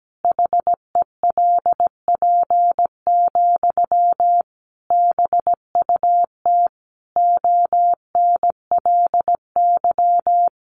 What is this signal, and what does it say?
Morse code, heard as a keyed tone, about 17 wpm, 700 Hz. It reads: HELP, BUT ONLY
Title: Sentences from Top 1500 Words 17wpm